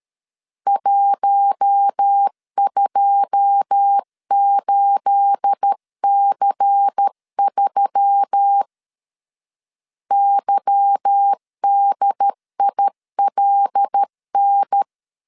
Morse code, heard as a keyed tone, about 13 wpm, 800 Hz. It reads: 128C3 YDILN
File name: LEZIONI / 9 28 12wpm 2351